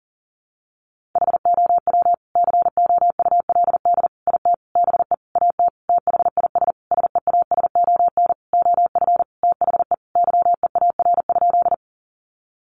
Morse code, heard as a keyed tone, about 40 wpm, 700 Hz. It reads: HOW COULD IT BE AT THIS SEASON OF THE YEAR?
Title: Sentences from Top 1400 Words 40wpm